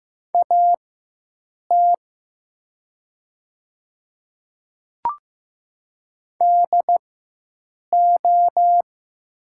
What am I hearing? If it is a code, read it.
Morse code, heard as a keyed tone, about 15 wpm, spaced as for 7 wpm, 700 Hz. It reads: AT DO
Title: 4x Character Spacing - Top 100 Words - 15wpm